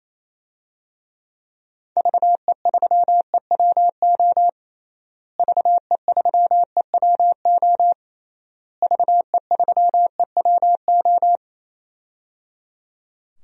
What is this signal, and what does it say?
Morse code, heard as a keyed tone, about 28 wpm, 700 Hz. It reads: VE3EWO VE3EWO VE3EWO
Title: DXCC-100 Call Signs - 6 Characters - Repeated 3x - 28wpm